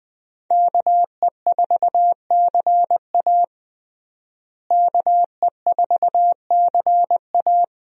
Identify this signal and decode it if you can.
Morse code, heard as a keyed tone, about 20 wpm, 700 Hz. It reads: KE4CA KE4CA
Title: USA Call Signs - 5 Characters - Repeated 2x - 20wpm